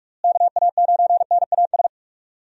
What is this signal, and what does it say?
Morse code, heard as a keyed tone, about 46 wpm, 700 Hz. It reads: KA9NAS
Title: USA Call Signs - 6 Characters - Repeated 2x - 45wpm